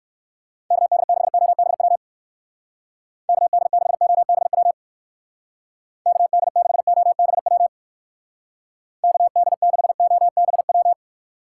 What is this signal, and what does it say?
Morse code, heard as a keyed tone, about 62 wpm, 700 Hz. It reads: KD6OBW KD6OBW KD6OBW KD6OBW